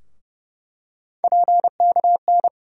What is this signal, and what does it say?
Morse code, heard as a keyed tone, about 30 wpm, 700 Hz. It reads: PKN
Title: USA Call Sign Postfixes - 3 Letters 30wpm